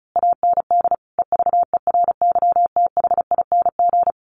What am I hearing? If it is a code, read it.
Morse code, heard as a keyed tone, about 34 wpm, 700 Hz. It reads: AND EVERYTHING
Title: A Sound Of Thunder 35wpm